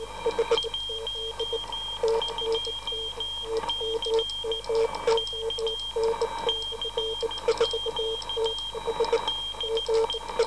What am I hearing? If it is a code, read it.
Morse code, heard as a keyed tone, about 14 wpm, 3200 Hz, with noise. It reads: 12C K CQ DE